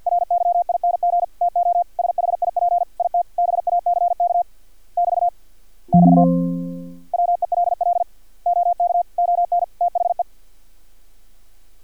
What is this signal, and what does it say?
Morse code, heard as a keyed tone, about 50 wpm, 700 Hz. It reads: RYING TO SHIP IT BACK <BT> <BT> WELL OK ON THE
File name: archived / CRSnet